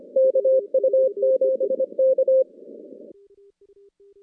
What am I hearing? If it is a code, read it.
Morse code, heard as a keyed tone, about 26 wpm, 530 Hz. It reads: KU7K
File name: contests / CW-SS-k5tr- -2200